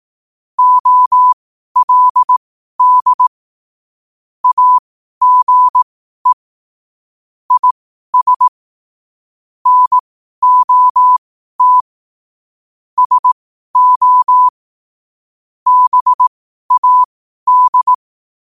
Morse code, heard as a keyed tone, about 18 wpm, spaced as for 12 wpm, 1000 Hz. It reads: OLD AGE IS NOT SO BAD
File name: qotdfiles / 12wpm